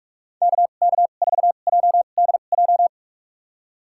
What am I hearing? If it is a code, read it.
Morse code, heard as a keyed tone, about 46 wpm, spaced as for 33 wpm, 700 Hz. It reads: KK4JDJ
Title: USA Call Signs - 6 Characters - 2x Character Spacing - 45wpm